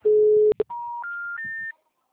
Special information tone.